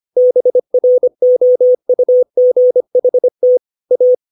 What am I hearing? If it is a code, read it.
Morse code, heard as a keyed tone, about 25 wpm, 500 Hz. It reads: BROUGHT A
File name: 632 words / 632 Words 25WPM